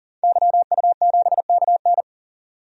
Morse code, heard as a keyed tone, about 40 wpm, 700 Hz. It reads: YU7KN
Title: DXCC-100 Call Signs - 5 Characters - 40wpm